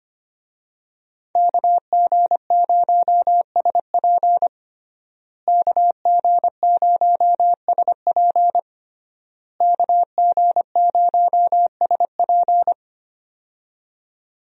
Morse code, heard as a keyed tone, about 25 wpm, 700 Hz. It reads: KG0SP KG0SP KG0SP
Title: USA Call Signs - 5 Characters - Repeated 3x - 25wpm